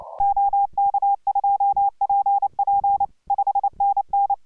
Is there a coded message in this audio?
Yes: Morse code from 0.2 to 4.3 s, heard as a keyed tone, about 29 wpm, 810 Hz. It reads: OK2PP 5NN